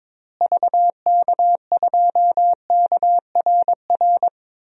Morse code, heard as a keyed tone, about 22 wpm, 700 Hz. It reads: VK2KRR
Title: DXCC-100 Call Signs - 6 Characters - Repeated 2x - 22wpm